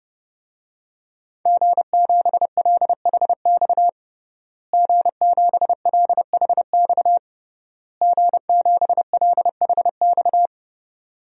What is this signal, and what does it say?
Morse code, heard as a keyed tone, about 30 wpm, 700 Hz. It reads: G7LHX G7LHX G7LHX
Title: DXCC-100 Call Signs - 5 Characters - Repeated 3x - 30wpm